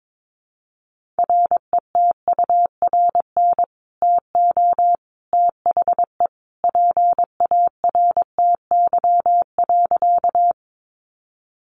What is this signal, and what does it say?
Morse code, heard as a keyed tone, about 22 wpm, 700 Hz. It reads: RETURN TO THE PARTY.
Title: Sentences from Top 1400 Words 22wpm